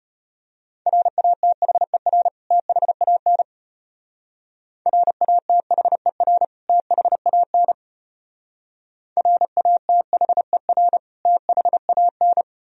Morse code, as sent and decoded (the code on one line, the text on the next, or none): .-. .- - .... .-. / - .... .- -. / .-. .- - .... .-. / - .... .- -. / .-. .- - .... .-. / - .... .- -.
RATHER THAN RATHER THAN RATHER THAN